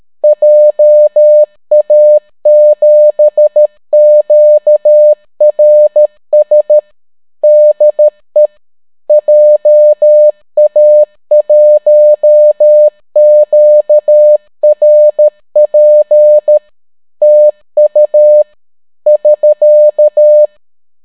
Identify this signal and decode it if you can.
Morse code, heard as a keyed tone, about 13 wpm, 600 Hz. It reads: JA7QRS DE JA1QRP TU <SK>